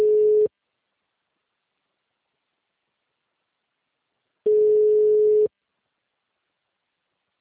Ringback tone (Europe, 425 Hz).
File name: out- -2006- - - .4047